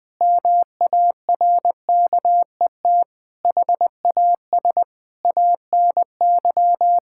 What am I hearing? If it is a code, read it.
Morse code, heard as a keyed tone, about 20 wpm, 700 Hz. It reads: MARKET HAS ANY
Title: Sentences from Top 1000 Words - Review Part 6 - 20wpm